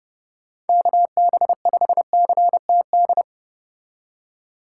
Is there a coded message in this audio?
Yes: Morse code from 0.7 to 3.2 s, heard as a keyed tone, about 30 wpm, 700 Hz. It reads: KB5CTD